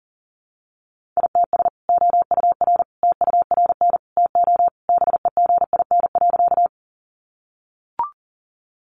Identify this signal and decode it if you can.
Morse code, heard as a keyed tone, about 40 wpm, 700 Hz. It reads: ITS OUR TURN TO BEGIN.
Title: Sentences from Top 600 Words 40wpm